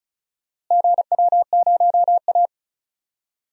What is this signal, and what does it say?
Morse code, heard as a keyed tone, about 36 wpm, 700 Hz. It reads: GW0A
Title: DXCC-100 Call Signs - 4 Characters - Repeated 2x - 35wpm